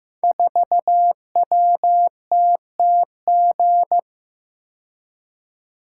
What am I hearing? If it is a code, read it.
Morse code, heard as a keyed tone, about 15 wpm, 700 Hz. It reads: 4WTTG
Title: Code Groups - 5 Letter:Numbers - 15wpm